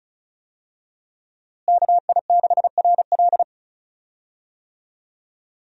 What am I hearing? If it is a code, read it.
Morse code, heard as a keyed tone, about 36 wpm, 700 Hz. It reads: KI6RL
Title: USA Call Signs - 5 Characters - Repeated 2x - 35wpm